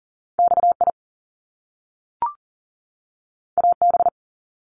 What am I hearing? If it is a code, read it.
Morse code, heard as a keyed tone, about 38 wpm, 700 Hz. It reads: XI AB
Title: Top Two Letter Combinations in Words 40wpm